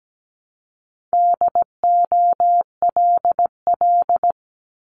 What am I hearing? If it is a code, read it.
Morse code, heard as a keyed tone, about 17 wpm, 700 Hz. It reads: DOLL
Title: Multi-letter Phonemes Repeated 3x 17wpm